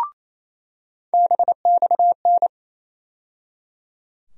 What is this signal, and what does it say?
Morse code, heard as a keyed tone, about 28 wpm, 700 Hz. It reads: BXN
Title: USA Call Sign Postfixes - 3 Letters 28wpm